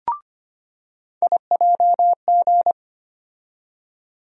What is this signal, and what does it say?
Morse code, heard as a keyed tone, about 25 wpm, 700 Hz. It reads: IJG
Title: USA Call Sign Postfixes - 3 Letters 25wpm